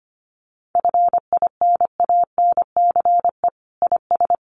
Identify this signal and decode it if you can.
Morse code, heard as a keyed tone, about 25 wpm, 700 Hz. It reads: FINANCE IS
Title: Sentences from Top 1500 Words 25wpm